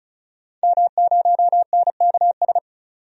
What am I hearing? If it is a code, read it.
Morse code, heard as a keyed tone, about 36 wpm, 700 Hz. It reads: M0NKS
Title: DXCC-100 Call Signs - 5 Characters - 35wpm